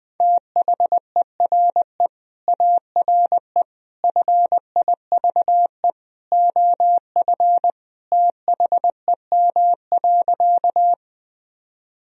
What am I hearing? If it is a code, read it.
Morse code, heard as a keyed tone, about 20 wpm, 700 Hz. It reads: THERE ARE FIVE OF THEM.
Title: Sentences from Top 1000 Words - Review Part 2 - 20wpm